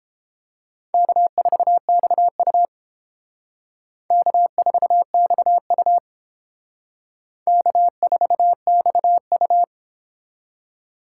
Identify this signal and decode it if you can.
Morse code, heard as a keyed tone, about 30 wpm, 700 Hz. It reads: K4XU K4XU K4XU